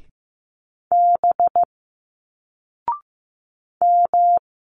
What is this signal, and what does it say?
Morse code, heard as a keyed tone, about 15 wpm, 700 Hz. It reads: B M
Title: Single Letters 15wpm